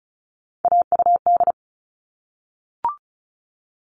Morse code, heard as a keyed tone, about 34 wpm, 700 Hz. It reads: AUD
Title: All Cut Numbers - 3-Digit 35wpm